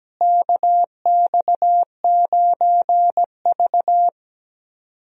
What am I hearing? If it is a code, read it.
Morse code, heard as a keyed tone, about 17 wpm, 700 Hz. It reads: KX9V
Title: USA Call Signs - 4 Characters - Repeated 2x - 17wpm